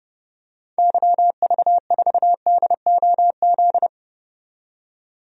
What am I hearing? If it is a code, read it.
Morse code, heard as a keyed tone, about 29 wpm, 700 Hz. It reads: YV4DOZ
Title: DXCC-100 Call Signs - 6 Characters - 30wpm